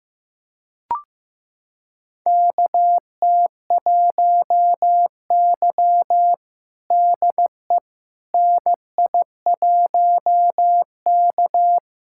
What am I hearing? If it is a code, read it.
Morse code, heard as a keyed tone, about 15 wpm, 700 Hz. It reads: KT1Y DE NI1K
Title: USA Callsign DE Callsign - 4 Characters - 15wpm